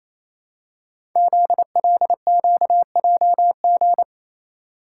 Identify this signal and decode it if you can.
Morse code, heard as a keyed tone, about 29 wpm, 700 Hz. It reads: ZLQJG
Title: Code Groups - 5 Letters - 28wpm